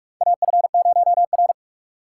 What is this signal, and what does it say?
Morse code, heard as a keyed tone, about 44 wpm, 700 Hz. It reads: AF0R
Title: USA Call Signs - 4 Characters - 3x Character Spacing - 45wpm